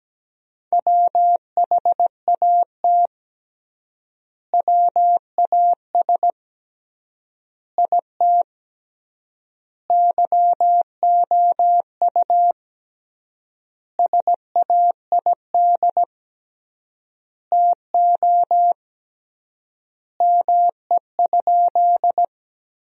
Morse code, heard as a keyed tone, about 17 wpm, 700 Hz. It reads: WHAT WAS IT YOU SAID TO ME?
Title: 3x Word-Spacing - Sentences from Top 200 Words - 17wpm